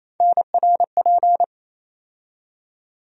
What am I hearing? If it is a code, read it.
Morse code, heard as a keyed tone, about 28 wpm, 700 Hz. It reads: NRP
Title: USA Call Sign Postfixes - 3 Letters 28wpm